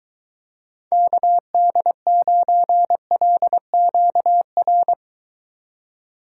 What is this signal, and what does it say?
Morse code, heard as a keyed tone, about 23 wpm, 700 Hz. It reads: KD9LQR